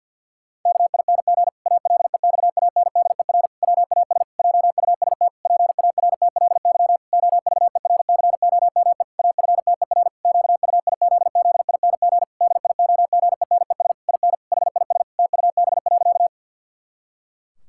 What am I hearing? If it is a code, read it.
Morse code, heard as a keyed tone, about 50 wpm, 700 Hz. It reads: KING ALEXANDER WAS JUST PARTLY OVERCOME AFTER QUIZZING DIOGENES IN HIS TUB.
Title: Pangram Challenge v2 - 50wpm